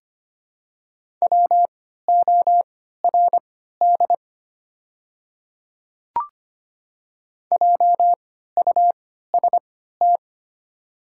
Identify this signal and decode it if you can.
Morse code, heard as a keyed tone, about 25 wpm, spaced as for 14 wpm, 690 Hz. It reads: WORD JUST
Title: 3x Character Spacing - Top 200 Words - 25wpm